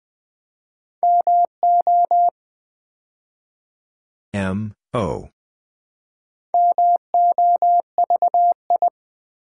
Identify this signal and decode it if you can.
Morse code, heard as a keyed tone, about 20 wpm, 700 Hz. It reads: MO MOVI
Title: Build A Word - 2 Letters At A Time - 6-Letter Words - 20wpm